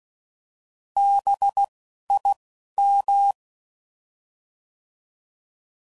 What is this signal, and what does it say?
Morse code, heard as a keyed tone, about 16 wpm, 800 Hz. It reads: B I M